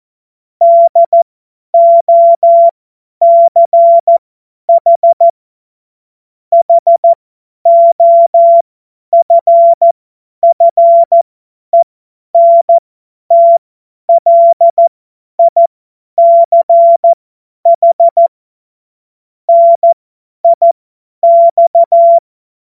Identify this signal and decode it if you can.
Morse code, heard as a keyed tone, about 14 wpm, spaced as for 10 wpm, 680 Hz. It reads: DOCH HOFFENTLICH NIX
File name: Morsekurs / Witze / Witz 01. August 2023 50 F70